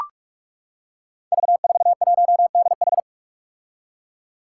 Morse code, heard as a keyed tone, about 42 wpm, 700 Hz. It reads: V41DH
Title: Five Challenging Letter-Number Combinations 45wpm